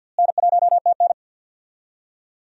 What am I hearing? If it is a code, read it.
Morse code, heard as a keyed tone, about 46 wpm, 700 Hz. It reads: N1TN
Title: USA Call Signs - 4 Characters - Repeated 2x - 50wpm